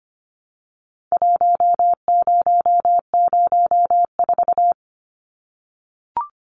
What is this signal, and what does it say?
Morse code, heard as a keyed tone, about 25 wpm, 700 Hz. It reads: 1004
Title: Numbers - Contest Style - 4-Digit 25wpm